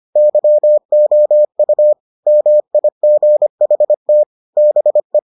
Morse code, heard as a keyed tone, about 25 wpm, 600 Hz. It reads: YOU MIGHT BE